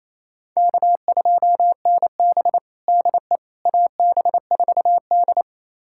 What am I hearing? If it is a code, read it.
Morse code, heard as a keyed tone, about 28 wpm, 700 Hz. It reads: K2NB DE AB4D
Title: USA Callsign DE Callsign - 4 Characters - 28wpm